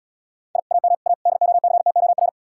Morse code, heard as a keyed tone, about 78 wpm, 700 Hz. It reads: I AM A DOCTOR